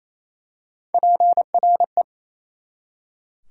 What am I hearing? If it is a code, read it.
Morse code, heard as a keyed tone, about 28 wpm, 700 Hz. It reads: PRE
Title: Prefixes and Suffixes 28wpm